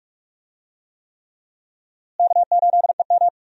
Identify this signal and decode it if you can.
Morse code, heard as a keyed tone, about 42 wpm, 700 Hz. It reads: K8EM
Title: USA Call Signs - 4 Characters - Repeated 3x - 45wpm